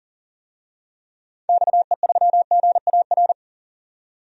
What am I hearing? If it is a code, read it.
Morse code, heard as a keyed tone, about 40 wpm, 700 Hz. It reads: XE3GAR
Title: DXCC-100 Call Signs - 6 Characters - 40wpm